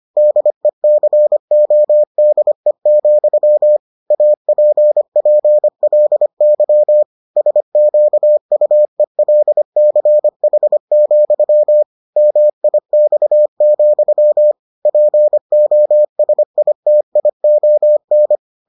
Morse code, heard as a keyed tone, about 25 wpm, 600 Hz. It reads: DECODE, APPLY SQUELCH, MIX, POSITION